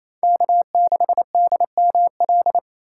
Morse code, heard as a keyed tone, about 28 wpm, 700 Hz. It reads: K6DML
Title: USA Call Signs - 5 Characters - 6x Character Spacing - 28wpm